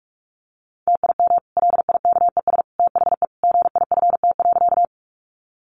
Morse code, heard as a keyed tone, about 44 wpm, 700 Hz. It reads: TIM LIKES THE GIFT.
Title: Easy Phrases 45wpm